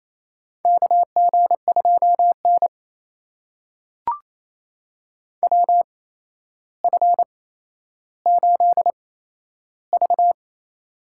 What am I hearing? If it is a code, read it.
Morse code, heard as a keyed tone, about 28 wpm, spaced as for 12 wpm, 700 Hz. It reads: KG2N WF8V
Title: USA Call Signs - 4 Characters - 8x Character Spacing - 28wpm